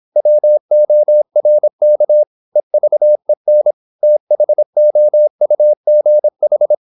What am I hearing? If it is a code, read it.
Morse code, heard as a keyed tone, about 26 wpm, 600 Hz. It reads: WORK EVEN THOUGH
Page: Never 26wpm | CW med Gnister